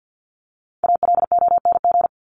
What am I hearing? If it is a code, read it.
Morse code, heard as a keyed tone, about 50 wpm, 700 Hz. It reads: ALONG
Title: Top 501-750 Words 50wpm